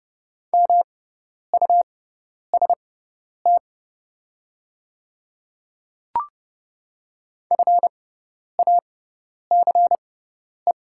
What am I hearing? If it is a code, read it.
Morse code, heard as a keyed tone, about 30 wpm, spaced as for 10 wpm, 690 Hz. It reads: MUST FACE
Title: 6x Character Spacing - Top 300 Words - 30wpm